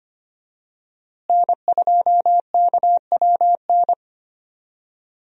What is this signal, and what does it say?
Morse code, heard as a keyed tone, about 25 wpm, 700 Hz. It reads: N2KWN
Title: USA Call Signs - 5 Characters - Repeated 3x - 25wpm